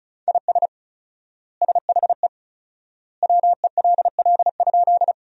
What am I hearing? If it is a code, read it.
Morse code, heard as a keyed tone, about 34 wpm, 700 Hz. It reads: IS SHE WELL?
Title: 4x Word-Spacing - Sentences from Top 200 Words - 35wpm